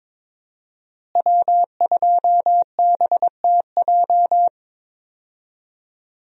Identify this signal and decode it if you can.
Morse code, heard as a keyed tone, about 21 wpm, 700 Hz. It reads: W2BTJ
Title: USA Call Signs - 5 Characters - Repeated 2x - 22wpm